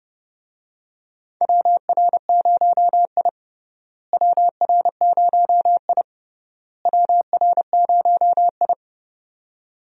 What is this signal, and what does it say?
Morse code, heard as a keyed tone, about 32 wpm, 700 Hz. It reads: WR0I WR0I WR0I
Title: USA Call Signs - 4 Characters - Repeated 3x - 30wpm